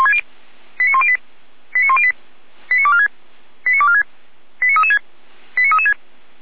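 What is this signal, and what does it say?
Five-tone selective call (ZVEI), heard as quick runs of five tones, tones of about 70 ms. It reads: 89108 89108 89246 89246 89336 89336
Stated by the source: ZVEI five-tone selective call received by SDR, feuerwehr-orlamuende fire brigade.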